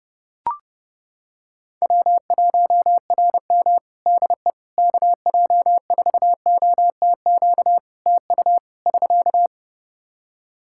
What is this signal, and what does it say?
Morse code, heard as a keyed tone, about 30 wpm, 700 Hz. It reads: W1RM DE KJ4OTQ TU <SK>
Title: Practice QSO - Exchange 4 - Ending - 30wpm